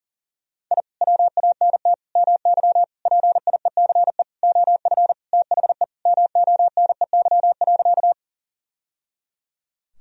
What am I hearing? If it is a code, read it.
Morse code, heard as a keyed tone, about 40 wpm, 700 Hz. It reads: I WANT MY PIECE OF THE MONEY.
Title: Sentences from Top 1000 Words - Review Part 2 - 40wpm